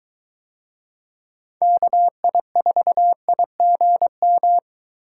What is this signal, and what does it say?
Morse code, heard as a keyed tone, about 23 wpm, 700 Hz. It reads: KI4IGM